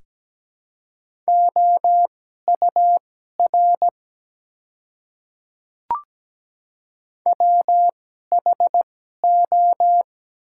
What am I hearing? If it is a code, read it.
Morse code, heard as a keyed tone, about 17 wpm, spaced as for 12 wpm, 700 Hz. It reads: OUR WHO
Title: 2x Character Spacing - Top 200 Words - 17wpm